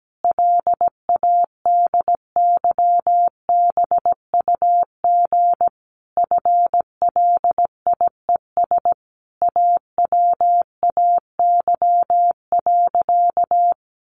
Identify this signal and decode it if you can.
Morse code, heard as a keyed tone, about 17 wpm, 700 Hz. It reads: LADYBUG FLIES AWAY.
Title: Additional Easy Phrases 17wpm